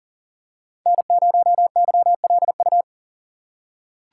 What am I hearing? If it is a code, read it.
Morse code, heard as a keyed tone, about 40 wpm, 700 Hz. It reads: N0YLU
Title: USA Call Signs - 5 Characters - 4x Character Spacing - 40wpm